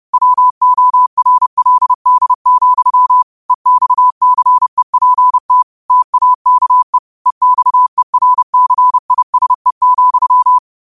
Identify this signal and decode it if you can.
Morse code, heard as a keyed tone, about 30 wpm, 1000 Hz. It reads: WORLD, EXCEPT TAKE EXERCISE,
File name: qotdfiles / 30wpm